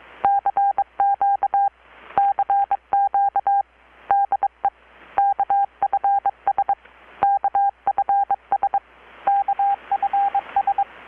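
Morse code, heard as a keyed tone, about 22 wpm, 800 Hz. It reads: CQ CQ DE KFS KFS KFS
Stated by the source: Morse from KPH's Night of Nights XX maritime radiotelegraphy event, received in Chiapas, Mexico about 3,800 km away on a Kenwood R-600 with a grounded 10 m long-wire antenna.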